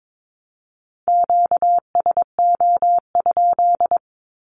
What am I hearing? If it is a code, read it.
Morse code, heard as a keyed tone, about 22 wpm, 700 Hz. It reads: QSO?
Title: Q-Codes 22wpm